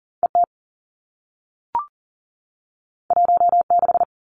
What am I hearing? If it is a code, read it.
Morse code, heard as a keyed tone, about 39 wpm, 700 Hz. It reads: ET 16
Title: Two Challenging Letter-Number Combinations 40wpm